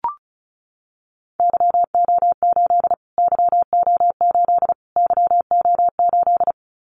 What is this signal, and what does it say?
Morse code, heard as a keyed tone, about 34 wpm, 700 Hz. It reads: YO8 YO8 YO8
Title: Callsign Prefixes for Top 25 DXCC - Repeated 3x - 35wpm